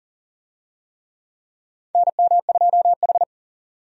Morse code, heard as a keyed tone, about 40 wpm, 700 Hz. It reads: NM2H
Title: USA Call Signs - 4 Characters - 40wpm